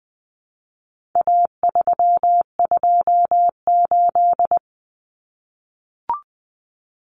Morse code, heard as a keyed tone, about 20 wpm, 700 Hz. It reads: A328
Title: Common Cut Numbers - 4-Digit 20wpm